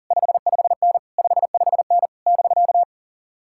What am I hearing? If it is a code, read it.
Morse code, heard as a keyed tone, about 53 wpm, spaced as for 47 wpm, 700 Hz. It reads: 55N 55N <BK>
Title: Speed-Racing Common QSO Phrases - 40wpm